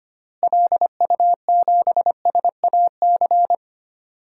Morse code, heard as a keyed tone, about 25 wpm, 700 Hz. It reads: LU7SAC
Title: DXCC-100 Call Signs - 6 Characters - Repeated 3x - 25wpm